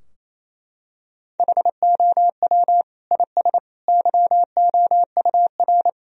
Morse code, heard as a keyed tone, about 27 wpm, 700 Hz. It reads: HOW IS YOUR
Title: Sentences from Top 1000 Words - Review Part 3 - 28wpm